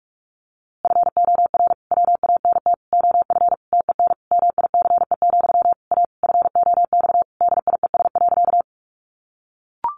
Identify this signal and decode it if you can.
Morse code, heard as a keyed tone, about 44 wpm, 700 Hz. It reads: FOR WANT OF TEN MICE, A FOX DIES.
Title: A Sound Of Thunder 45wpm